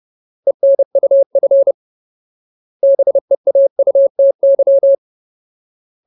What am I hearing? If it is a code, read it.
Morse code, heard as a keyed tone, about 30 wpm, 550 Hz. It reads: ENUF BEAUTY
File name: Ultimate Word List / 30WPM Ultimate